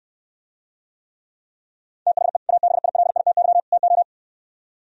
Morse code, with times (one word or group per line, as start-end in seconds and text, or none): 2.1-2.4 THE
2.5-3.6 ABILITY
3.7-4.0 TO